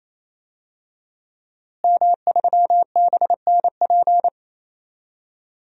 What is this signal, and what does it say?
Morse code, heard as a keyed tone, about 27 wpm, 700 Hz. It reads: M3BNP
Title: DXCC-100 Call Signs - 5 Characters - Repeated 3x - 28wpm